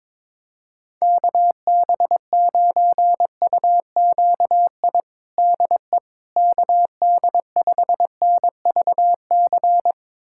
Morse code, heard as a keyed tone, about 22 wpm, 700 Hz. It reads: KB9UQI DE KD5NVC
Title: USA Callsign DE Callsign - 6 Characters - 22wpm